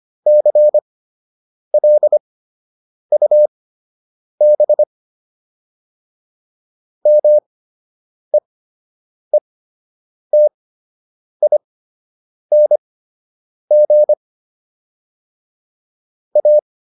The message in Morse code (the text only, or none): CLUB MEETING A